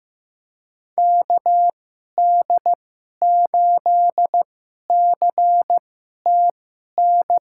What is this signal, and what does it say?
Morse code, heard as a keyed tone, about 15 wpm, spaced as for 11 wpm, 700 Hz. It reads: KD8CTN
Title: USA Call Signs - 6 Characters - 2x Character Spacing - 15wpm